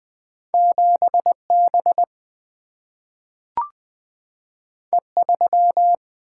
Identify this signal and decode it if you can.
Morse code, heard as a keyed tone, about 20 wpm, 700 Hz. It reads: 7 B E 3
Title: Single-Letter Single-Number 20wpm